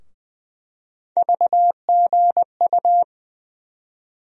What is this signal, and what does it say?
Morse code, heard as a keyed tone, about 20 wpm, 700 Hz. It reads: VGU
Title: USA Call Sign Postfixes - 3 Letters 20wpm